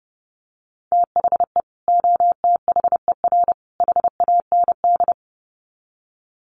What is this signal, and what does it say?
Morse code, heard as a keyed tone, about 30 wpm, 700 Hz. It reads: THE OTHER HAND
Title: Sets of 3 Words 30wpm